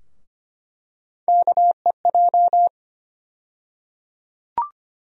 Morse code, heard as a keyed tone, about 25 wpm, 700 Hz. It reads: KEJ
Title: USA Call Sign Postfixes - 3 Letters 25wpm